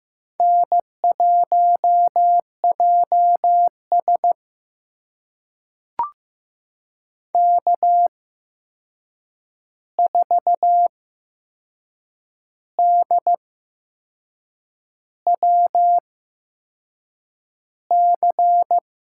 Morse code, heard as a keyed tone, about 15 wpm, spaced as for 6 wpm, 700 Hz. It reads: N1JS K4DWC
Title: USA Call Signs - 8x Character Spacing - 15wpm